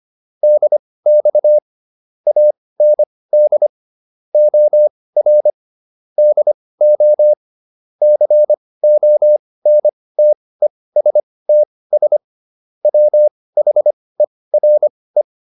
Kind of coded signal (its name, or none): Morse code